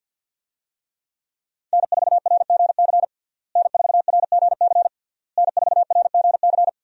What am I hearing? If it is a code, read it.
Morse code, heard as a keyed tone, about 52 wpm, 700 Hz. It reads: N4RGC N4RGC N4RGC
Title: USA Call Signs - 5 Characters - Repeated 3x - 50wpm